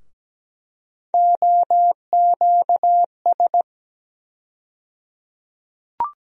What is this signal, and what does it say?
Morse code, heard as a keyed tone, about 17 wpm, 700 Hz. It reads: OQS
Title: USA Call Sign Postfixes - 3 Letters 17wpm